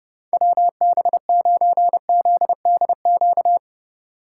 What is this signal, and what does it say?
Morse code, heard as a keyed tone, about 31 wpm, 700 Hz. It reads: WB9ZDQ
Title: Speed Racing - USA Call Signs 25wpm